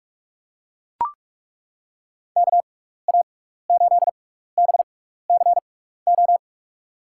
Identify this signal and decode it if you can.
Morse code, heard as a keyed tone, about 42 wpm, spaced as for 15 wpm, 700 Hz. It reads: KA8BCO
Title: USA Call Signs - 6 Characters - 6x Character Spacing - 45wpm